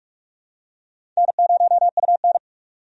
Morse code, heard as a keyed tone, about 46 wpm, 700 Hz. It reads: N0UN
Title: USA Call Signs - 4 Characters - 45wpm